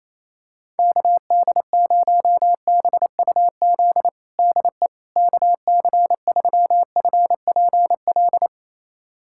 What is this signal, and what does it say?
Morse code, heard as a keyed tone, about 28 wpm, 700 Hz. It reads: KD0BUZ DE KC3FPL
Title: USA Callsign DE Callsign - 6 Characters - 28wpm